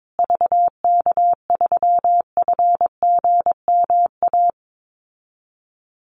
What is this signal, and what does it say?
Morse code, heard as a keyed tone, about 22 wpm, 700 Hz. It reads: VK3FGMA
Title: USA Call Signs Repeated 2x 22wpm